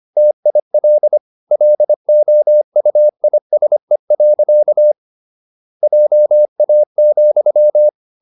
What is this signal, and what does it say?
Morse code, heard as a keyed tone, about 25 wpm, 600 Hz. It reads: TIL LOUISE. JA,
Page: Ferie 25 wpm